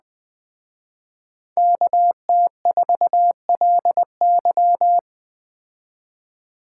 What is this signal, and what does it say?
Morse code, heard as a keyed tone, about 18 wpm, 700 Hz. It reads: KT4LY
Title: Speed Racing - USA 2x2 Call Signs 17wpm